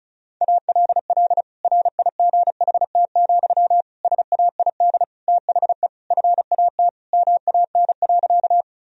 Morse code, heard as a keyed tone, about 34 wpm, 700 Hz. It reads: ALL RIGHT, SAID THE FAT MAN.
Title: Sentences from Top 1000 Words - Review Part 4 - 35wpm